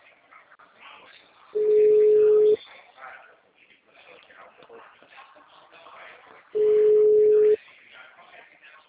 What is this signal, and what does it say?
Ringback tone (Europe, 425 Hz), with speech.